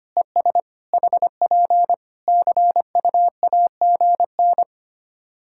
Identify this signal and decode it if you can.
Morse code, heard as a keyed tone, about 25 wpm, 700 Hz. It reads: ES HP CUAGN
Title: K1USN SST QSO Practice 25wpm